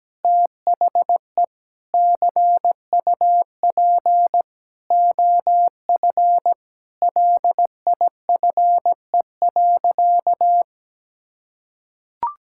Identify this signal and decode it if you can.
Morse code, heard as a keyed tone, about 17 wpm, 700 Hz. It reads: THE CUP OF LIFE.
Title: Sentences from Top 1600 Words 17wpm